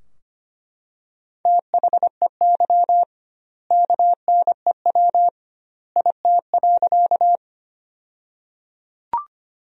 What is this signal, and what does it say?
Morse code, heard as a keyed tone, about 25 wpm, 700 Hz. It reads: THEY KNEW IT.
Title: 2x Word-Spacing - Sentences from Top 400 Words - 25wpm